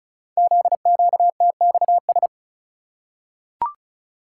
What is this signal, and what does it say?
Morse code, heard as a keyed tone, about 36 wpm, 700 Hz. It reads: ZQTXS
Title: Code Groups - 5 Letters - 35wpm